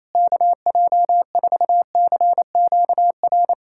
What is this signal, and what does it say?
Morse code, heard as a keyed tone, about 27 wpm, 700 Hz. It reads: KJ4CQR
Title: USA Callsign DE Callsign 28wpm